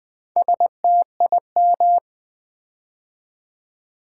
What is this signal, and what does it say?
Morse code, heard as a keyed tone, about 20 wpm, 700 Hz. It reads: STIM
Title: Build A Word - 4 Letters At A Time - 8-Letter Words - 20wpm